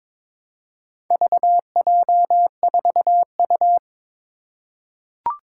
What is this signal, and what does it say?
Morse code, heard as a keyed tone, about 23 wpm, 700 Hz. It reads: VJ4U
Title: Four Challenging Letter-Number Combinations 22wpm